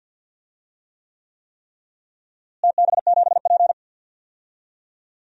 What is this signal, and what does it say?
Morse code, heard as a keyed tone, about 52 wpm, 700 Hz. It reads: TB7P
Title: DXCC-100 Call Signs - 4 Characters - Repeated 3x - 50wpm